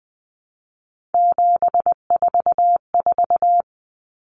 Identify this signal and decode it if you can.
Morse code, heard as a keyed tone, about 20 wpm, 700 Hz. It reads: 744
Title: Common Cut Numbers - 3-Digit 20wpm